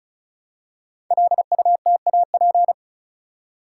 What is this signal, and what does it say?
Morse code, heard as a keyed tone, about 36 wpm, 690 Hz. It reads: LUTAP
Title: Code Groups - 5 Letters - 35wpm